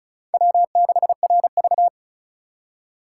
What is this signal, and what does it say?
Morse code, heard as a keyed tone, about 34 wpm, 700 Hz. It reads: W6RV
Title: USA Call Signs - 4 Characters - 3x Character Spacing - 35wpm